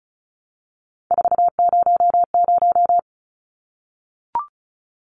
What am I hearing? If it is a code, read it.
Morse code, heard as a keyed tone, about 34 wpm, 700 Hz. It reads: 400
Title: Numbers - 3-Digit 35wpm